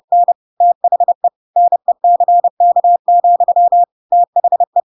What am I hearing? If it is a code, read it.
Morse code, heard as a keyed tone, about 30 wpm, 700 Hz. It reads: N THE NECK, THE
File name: Pinocchio / Pinocchio ch3 30 30 ik1hge